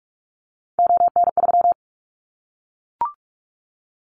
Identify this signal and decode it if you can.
Morse code, heard as a keyed tone, about 44 wpm, 700 Hz. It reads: ON3
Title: Callsign Prefixes for Top 25 DXCC - Repeated 3x - 45wpm